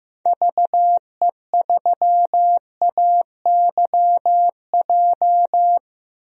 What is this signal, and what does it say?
Morse code, heard as a keyed tone, about 16 wpm, 700 Hz. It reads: VE3AYJ
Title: DXCC-100 Call Signs - 6 Characters - 15wpm